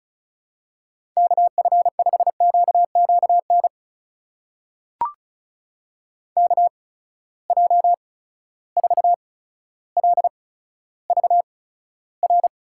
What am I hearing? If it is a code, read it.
Morse code, heard as a keyed tone, about 36 wpm, spaced as for 15 wpm, 700 Hz. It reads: KF5QQN KJ4LVR
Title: USA Call Signs - 6 Characters - 8x Character Spacing - 35wpm